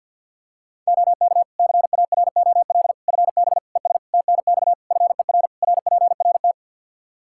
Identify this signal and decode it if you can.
Morse code, heard as a keyed tone, about 46 wpm, 700 Hz. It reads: OK CAROL FB ES TNX FER RPRT
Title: K1USN SST QSO Practice 50wpm